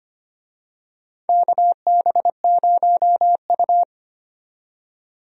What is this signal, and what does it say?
Morse code, heard as a keyed tone, about 25 wpm, 700 Hz. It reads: KB0U
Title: USA Call Signs - 4 Characters - 25wpm